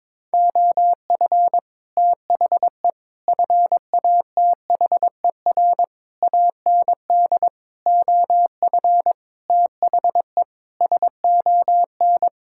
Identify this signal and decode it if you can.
Morse code, heard as a keyed tone, about 22 wpm, 700 Hz. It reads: OF THE FATHER AND OF THE SON
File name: Sets of 9 Words - 22wpm